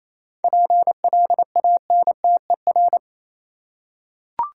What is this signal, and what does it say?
Morse code, heard as a keyed tone, about 27 wpm, 700 Hz. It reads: PLANTER
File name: Easy Words - Speed-Racing with Hints - 22wpm